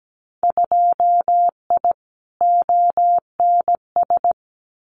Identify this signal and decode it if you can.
Morse code, heard as a keyed tone, about 17 wpm, 700 Hz. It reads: 2I ONS
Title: Winter Field Day Exchanges Encore 17wpm